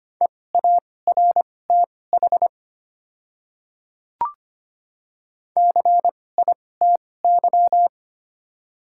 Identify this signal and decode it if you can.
Morse code, heard as a keyed tone, about 25 wpm, spaced as for 18 wpm, 700 Hz. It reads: EARTH CITY
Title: 2x Character Spacing - Top 300 Words - 25wpm